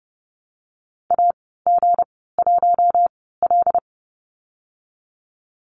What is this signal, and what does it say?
Morse code, heard as a keyed tone, about 32 wpm, spaced as for 17 wpm, 700 Hz. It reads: AG1L
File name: Lesson 37g - Letter x - Call Signs 30wpm